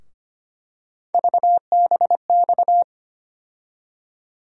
Morse code, heard as a keyed tone, about 25 wpm, 700 Hz. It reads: VBX
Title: USA Call Sign Postfixes - 3 Letters 25wpm